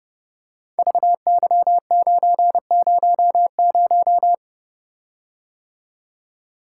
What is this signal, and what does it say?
Morse code, heard as a keyed tone, about 30 wpm, 700 Hz. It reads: VY900
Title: Code Groups - 5 Letter:Numbers - 30wpm